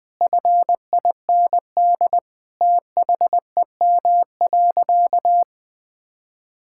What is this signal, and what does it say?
Morse code, heard as a keyed tone, about 20 wpm, 700 Hz. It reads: FIND THEM.
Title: Sentences from Top 1000 Words - Review Part 6 - 20wpm